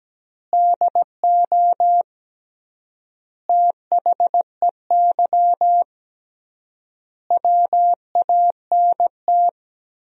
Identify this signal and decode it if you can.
Morse code, heard as a keyed tone, about 17 wpm, 700 Hz. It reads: DO THEY WANT
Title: 3x Word-Spacing - Sentences from Top 400 Words - 17wpm